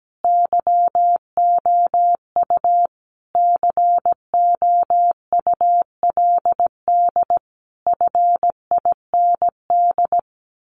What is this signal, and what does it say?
Morse code, heard as a keyed tone, about 17 wpm, 700 Hz. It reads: YOU COULD FIND
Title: Sentences from Top 1400 Words 17wpm